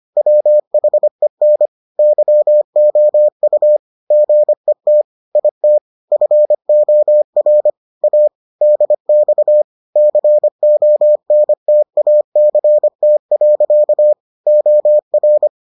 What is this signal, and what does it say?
Morse code, heard as a keyed tone, about 25 wpm, 600 Hz. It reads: WHEN YOU GET IT FOR A DX CONTACT. OR